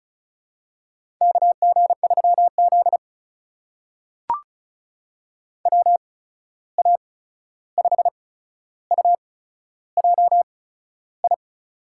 Morse code, heard as a keyed tone, about 36 wpm, spaced as for 13 wpm, 700 Hz. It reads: KG3Z WA5UJI
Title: USA Call Signs - 8x Character Spacing - 35wpm